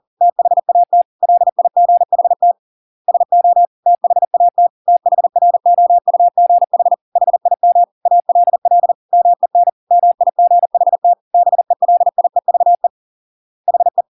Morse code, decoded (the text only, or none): THAT LIGHT SO THAT THROUGH HIM ALL MEN MIGHT BELIEVE HE